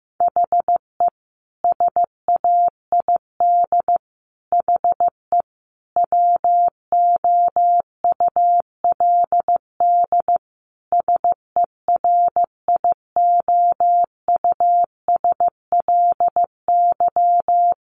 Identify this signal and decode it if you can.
Morse code, heard as a keyed tone, about 15 wpm, 700 Hz. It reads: HE SAID HE WOULD SERIOUSLY
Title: Sentences from Top 1200 Words 15wpm